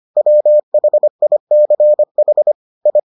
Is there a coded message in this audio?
Yes: Morse code from 0.2 to 3.0 s, heard as a keyed tone, about 25 wpm, 600 Hz. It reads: WHICH I